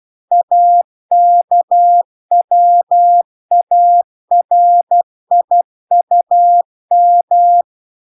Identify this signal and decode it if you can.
Morse code, heard as a keyed tone, about 12 wpm, 700 Hz. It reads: AKWARIUM